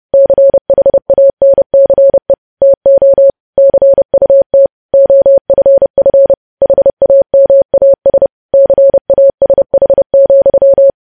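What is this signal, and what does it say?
Morse code, heard as a keyed tone, about 30 wpm, 550 Hz. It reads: CHANCE TO CUT OFF HAMAS CASH,
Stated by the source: Morse code news headlines practice at 30 WPM.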